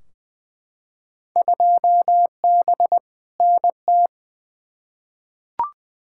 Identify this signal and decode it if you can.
Morse code, heard as a keyed tone, about 20 wpm, 700 Hz. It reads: 2B NT
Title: Field Day Exchanges Encore - 20wpm